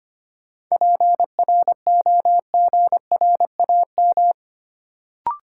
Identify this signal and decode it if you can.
Morse code, heard as a keyed tone, about 25 wpm, 700 Hz. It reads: PROGRAM
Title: Sentences from Top 1500 Words - New Word Drill Alt - 25wpm